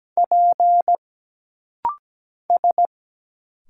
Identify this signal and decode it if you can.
Morse code, heard as a keyed tone, about 17 wpm, 700 Hz. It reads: P S
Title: Single Letter-Number Alt - Rapid-Fire 17wpm